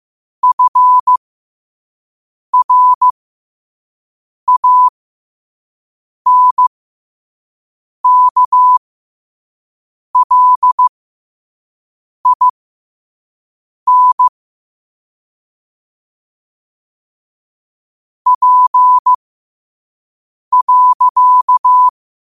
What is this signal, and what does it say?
Morse code, heard as a keyed tone, about 15 wpm, spaced as for 5 wpm, 1000 Hz. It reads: FRANKLIN P.
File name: qotdfiles / 5wpm